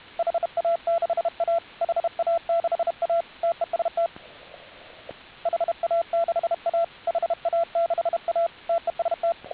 Morse code, heard as a keyed tone, about 32 wpm, 680 Hz, with radio noise. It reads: HA6A HA6A TEST HA6A HA6A TEST